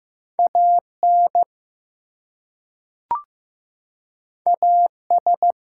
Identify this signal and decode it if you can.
Morse code, heard as a keyed tone, about 15 wpm, 700 Hz. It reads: AN AS
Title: Two Letter Words - 15wpm